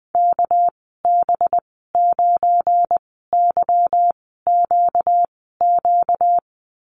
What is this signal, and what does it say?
Morse code, heard as a keyed tone, about 20 wpm, spaced as for 14 wpm, 700 Hz. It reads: KB9YQQ
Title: 245 Lesson 39g - Letter z - Call Signs 20wpm